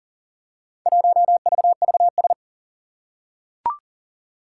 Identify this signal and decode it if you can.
Morse code, heard as a keyed tone, about 40 wpm, 700 Hz. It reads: 1VVS
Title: Four Challenging Letter-Number Combinations 40wpm